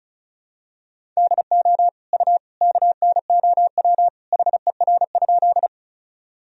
Morse code, heard as a keyed tone, about 36 wpm, 700 Hz. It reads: DO U KNOW HER?
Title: Sentences from Top 100 Words and Abbreviations 35wpm